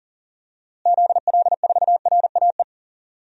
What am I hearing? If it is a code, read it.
Morse code, heard as a keyed tone, about 40 wpm, 700 Hz. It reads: ZL4RAE
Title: DXCC-100 Call Signs - 6 Characters - Repeated 3x - 40wpm